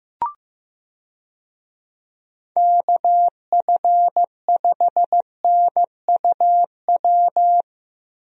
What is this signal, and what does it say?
Morse code, heard as a keyed tone, about 15 wpm, 700 Hz. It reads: KF5NUW
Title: USA Call Signs - 6 Characters - Repeated 2x - 15wpm